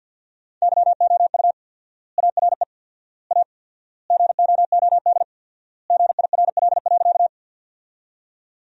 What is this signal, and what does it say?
Morse code, heard as a keyed tone, about 50 wpm, 700 Hz. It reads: YOU ARE A GOOD GIRL.
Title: 4x Word-Spacing - Sentences from Top 400 Words - 50wpm